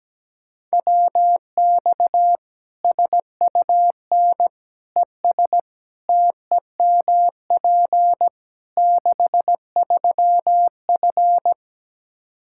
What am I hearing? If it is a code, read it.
Morse code, heard as a keyed tone, about 17 wpm, 700 Hz. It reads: WX SUN ES TEMP 63F
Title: Practice QSO - Exchange 2 - Info - 17wpm